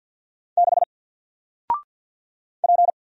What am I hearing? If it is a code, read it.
Morse code, heard as a keyed tone, about 50 wpm, 700 Hz. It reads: 6 P
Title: Single Character (Letter:Number:Punctuation:Prosign) Alt 50wpm